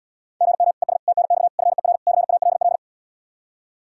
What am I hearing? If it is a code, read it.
Morse code, heard as a keyed tone, about 78 wpm, 700 Hz. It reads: GM ES TNX FER CALL